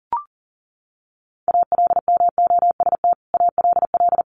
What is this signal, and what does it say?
Morse code, heard as a keyed tone, about 40 wpm, 700 Hz. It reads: ALMOST ALL
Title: Sentences from Top 1500 Words 40wpm